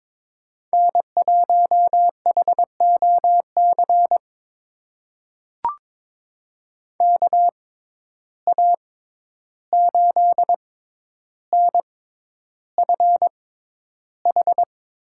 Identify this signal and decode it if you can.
Morse code, heard as a keyed tone, about 22 wpm, spaced as for 11 wpm, 700 Hz. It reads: N1HOC KA8NFH
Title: USA Call Signs - 6x Character Spacing - 22wpm